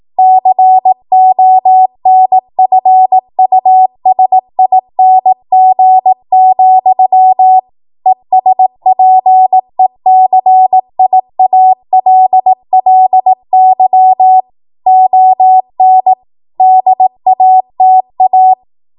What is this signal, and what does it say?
Morse code, heard as a keyed tone, about 18 wpm, 750 Hz. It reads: CONFUSING, ESPECIALLY ON DATA
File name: Morse / 18WPM